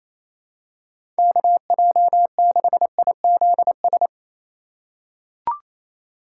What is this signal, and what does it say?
Morse code, heard as a keyed tone, about 28 wpm, 700 Hz. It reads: KJ6IZS
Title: USA Call Signs - 6 Characters - 3x Character Spacing - 28wpm